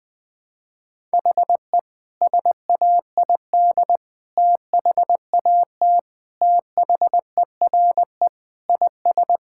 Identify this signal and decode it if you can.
Morse code, heard as a keyed tone, about 20 wpm, 700 Hz. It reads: HE SAID THAT THERE IS
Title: Sentences with 8-Word Phrases - Spoken Beginning - 20wpm